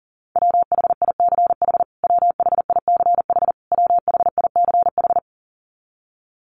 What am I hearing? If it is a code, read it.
Morse code, heard as a keyed tone, about 40 wpm, 700 Hz. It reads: WHICH WHICH WHICH
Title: Top 200 Words Repeated 40wpm